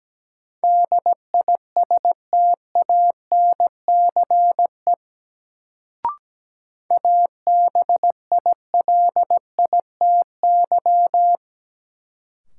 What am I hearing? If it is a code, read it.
Morse code, heard as a keyed tone, about 17 wpm, 700 Hz. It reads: DISTANCE ABILITY
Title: Sentences from Top 1200 Words - New Word Drill Alt - 17wpm